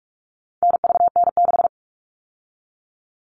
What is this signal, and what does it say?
Morse code, heard as a keyed tone, about 44 wpm, 700 Hz. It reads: NVN6
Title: All Cut Numbers - 4-Digit 45wpm